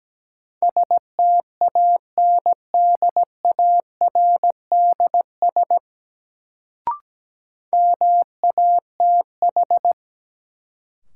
Morse code, heard as a keyed tone, about 17 wpm, 700 Hz. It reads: STANDARDS MATH
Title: Sentences from Top 1000 Words - Review Part 6 - New Word Drill - 17wpm